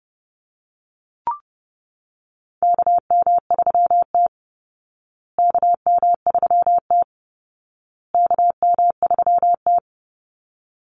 Morse code, heard as a keyed tone, about 30 wpm, 700 Hz. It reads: KM3T KM3T KM3T
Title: Call Signs Repeated 2x - 30wpm